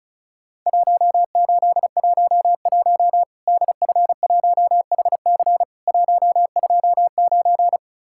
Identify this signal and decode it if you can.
Morse code, heard as a keyed tone, about 36 wpm, 700 Hz. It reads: 1811 DF1HC 129
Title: Worked All Europe DX Contest - QTC Exchange - 35wpm